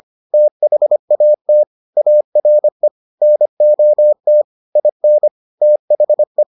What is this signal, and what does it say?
Morse code, heard as a keyed tone, about 25 wpm, 600 Hz. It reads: THAT ARE NOT IN THE